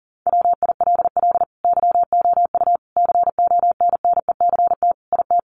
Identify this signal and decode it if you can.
Morse code, heard as a keyed tone, about 40 wpm, 700 Hz. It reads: WILL YOU CONNECT IT
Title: Sentences from Top 1500 Words 40wpm